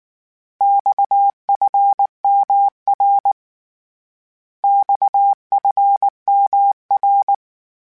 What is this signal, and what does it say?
Morse code, heard as a keyed tone, about 19 wpm, 800 Hz. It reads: XFMR XFMR
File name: Abreviations / Abreviations mp3s / Abreviations@18